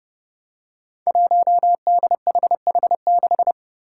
Morse code, heard as a keyed tone, about 30 wpm, 700 Hz. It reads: 1DHH6
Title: Five Challenging Letter-Number Combinations 30wpm